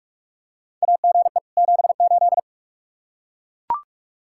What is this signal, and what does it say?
Morse code, heard as a keyed tone, about 44 wpm, 700 Hz. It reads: AGE 78
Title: Practice QSO - Exchange 3 - More Info - 45wpm